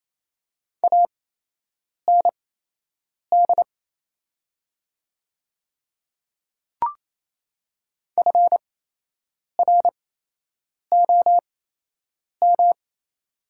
Morse code, heard as a keyed tone, about 28 wpm, spaced as for 8 wpm, 700 Hz. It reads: AND FROM
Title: 8x Character Spacing - Top 100 Words - 28wpm